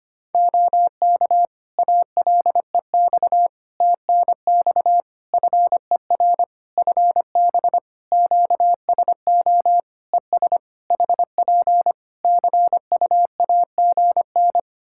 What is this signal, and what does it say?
Morse code, heard as a keyed tone, about 25 wpm, 700 Hz. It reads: OK ALEX TNX FER FB QSO ES HP CUAGN
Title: Practice QSO - Exchange 4 - Ending - 25wpm